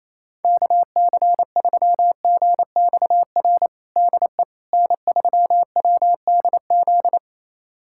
Morse code, heard as a keyed tone, about 28 wpm, 700 Hz. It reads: KC3GXR DE N3WDZ
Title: USA Callsign DE Callsign 28wpm